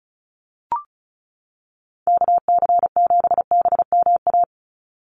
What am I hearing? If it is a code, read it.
Morse code, heard as a keyed tone, about 34 wpm, 700 Hz. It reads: KC7BMA
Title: USA Call Signs 35wpm